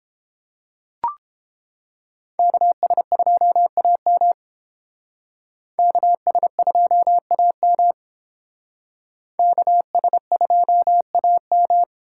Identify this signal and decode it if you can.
Morse code, heard as a keyed tone, about 28 wpm, 700 Hz. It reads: KS2AM KS2AM KS2AM